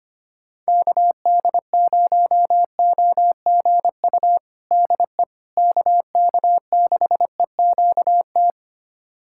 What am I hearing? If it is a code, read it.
Morse code, heard as a keyed tone, about 25 wpm, 700 Hz. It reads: KD0OGU DE KK6EQT
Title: USA Callsign DE Callsign - 6 Characters - 25wpm